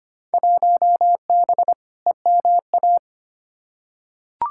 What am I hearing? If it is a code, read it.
Morse code, heard as a keyed tone, about 25 wpm, 700 Hz. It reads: 1B EMA
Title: Field Day Exchanges - 25wpm